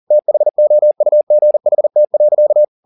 Morse code, heard as a keyed tone, about 40 wpm, 600 Hz. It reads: THOUGHT.
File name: the old man and the sea 2 40wpm